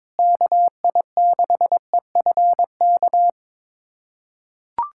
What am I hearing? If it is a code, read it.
Morse code, heard as a keyed tone, about 22 wpm, 700 Hz. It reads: KI6EFK
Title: USA Callsign DE Callsign - 6 Characters - 22wpm